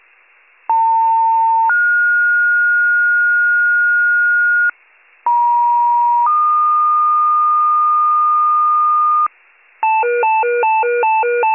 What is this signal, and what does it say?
Two-tone page, heard as a one-second tone, then a three-second tone, about 900 then 1430 Hz, about 950 then 1220 Hz, with noise. It is heard from 0.7 to 9.3 s.